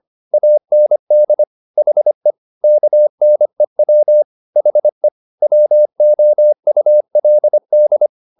Morse code, heard as a keyed tone, about 25 wpm, 600 Hz. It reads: AND HE KNEW HE WOULD